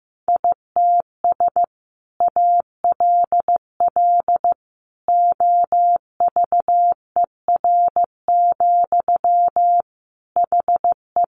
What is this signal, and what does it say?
Morse code, heard as a keyed tone, about 15 wpm, 700 Hz. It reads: ITS ALL OVER, HE
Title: Sentences from Top 600 Words 15wpm